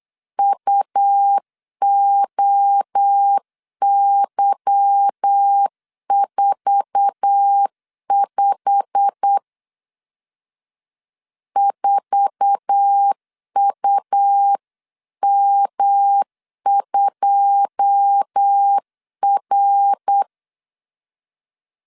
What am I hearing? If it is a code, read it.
Morse code, heard as a keyed tone, about 8 wpm, 800 Hz. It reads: UOY45 4UM2R
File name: LEZIONI / 9 2 8wpm 3437